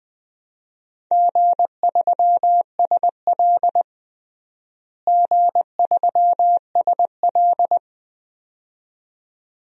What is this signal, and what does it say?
Morse code, heard as a keyed tone, about 20 wpm, 700 Hz. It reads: G3SL G3SL
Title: DXCC-100 Call Signs - 4 Characters - Repeated 2x - 20wpm